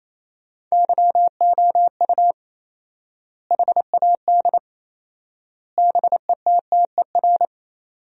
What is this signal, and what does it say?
Morse code, heard as a keyed tone, about 28 wpm, 700 Hz. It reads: YOU HAD BETTER
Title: 4x Word-Spacing - Sentences from Top 400 Words - 28wpm